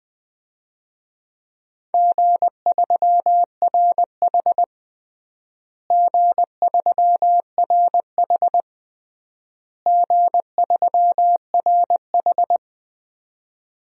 Morse code, heard as a keyed tone, about 21 wpm, 700 Hz. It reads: G3RH G3RH G3RH
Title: DXCC-100 Call Signs - 4 Characters - Repeated 3x - 20wpm